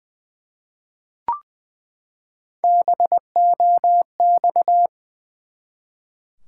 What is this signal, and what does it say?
Morse code, heard as a keyed tone, about 21 wpm, 700 Hz. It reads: BOX
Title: Three Letter Words - 20wpm